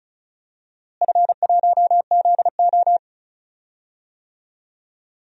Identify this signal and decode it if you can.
Morse code, heard as a keyed tone, about 36 wpm, 700 Hz. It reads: F1ZO
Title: DXCC-100 Call Signs - 4 Characters - Repeated 2x - 35wpm